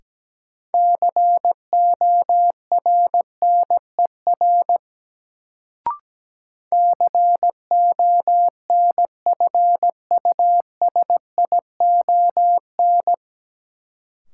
Morse code, heard as a keyed tone, about 17 wpm, 700 Hz. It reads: CORNER CONFUSION
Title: Sentences from Top 1600 Words - New Word Drill - 17wpm